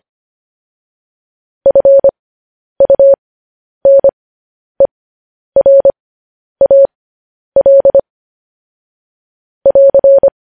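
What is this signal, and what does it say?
Morse code, heard as a keyed tone, about 25 wpm, spaced as for 10 wpm, 550 Hz. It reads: FUNERAL <AR>
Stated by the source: Morse code transmission of news headlines.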